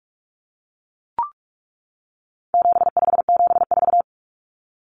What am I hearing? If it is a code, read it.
Morse code, heard as a keyed tone, about 44 wpm, 700 Hz. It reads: 7574
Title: Common Cut Numbers - 4-Digit 45wpm